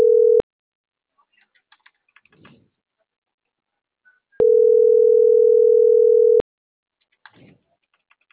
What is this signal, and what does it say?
Ringback tone (North America), 440 + 480 Hz.